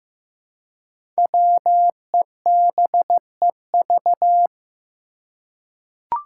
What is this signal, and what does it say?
Morse code, heard as a keyed tone, about 15 wpm, 700 Hz. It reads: WEBEV
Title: Five Challenging Letter-Number Combinations 15wpm